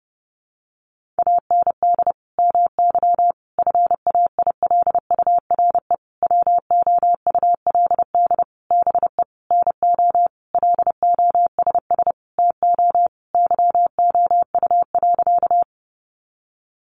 Morse code, heard as a keyed tone, about 30 wpm, 700 Hz. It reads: AND MY FAILURE WOULD BE NO LOSS TO YOU.
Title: Sentences from Top 1400 Words 30wpm